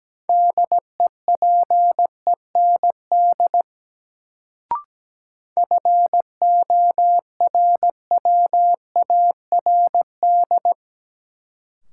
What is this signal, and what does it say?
Morse code, heard as a keyed tone, about 17 wpm, 700 Hz. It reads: DEPEND FORWARD
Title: Sentences from Top 1300 Words - New Word Drill - 17wpm